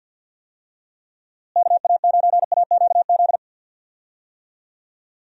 Morse code, heard as a keyed tone, about 47 wpm, 700 Hz. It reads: KA9AQZ
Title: USA Call Signs - 6 Characters - Repeated 3x - 50wpm